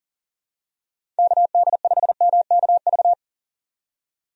Morse code, heard as a keyed tone, about 38 wpm, 700 Hz. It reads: KD5MKV
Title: USA Call Signs - 6 Characters - 40wpm